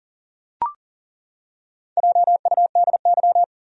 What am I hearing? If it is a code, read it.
Morse code, heard as a keyed tone, about 40 wpm, 700 Hz. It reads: JUDY
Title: USA Common Names - Max 5 Letters - 40wpm